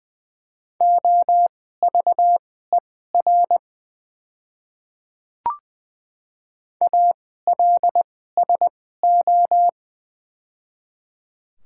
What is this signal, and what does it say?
Morse code, heard as a keyed tone, about 20 wpm, spaced as for 15 wpm, 690 Hz. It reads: OVER ALSO
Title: 2x Character Spacing - Top 200 Words - 20wpm